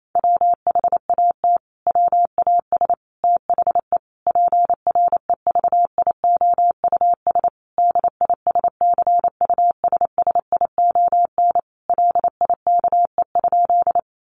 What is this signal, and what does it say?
Morse code, heard as a keyed tone, about 28 wpm, 700 Hz. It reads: WHAT WAS THE PREVIOUS DISCUSSION LIKE?
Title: Sentences from Top 1500 Words 28wpm